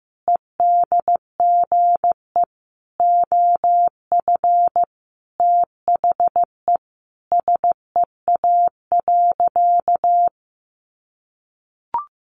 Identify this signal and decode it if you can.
Morse code, heard as a keyed tone, about 15 wpm, 700 Hz. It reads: EDGE OF THE SEA.
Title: Sentences from Top 1300 Words 15wpm